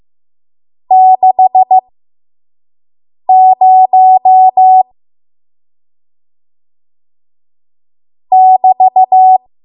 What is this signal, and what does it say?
Morse code, heard as a keyed tone, about 15 wpm, spaced as for 5 wpm, 750 Hz. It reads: 60 <BT>
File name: Morse / 05WPM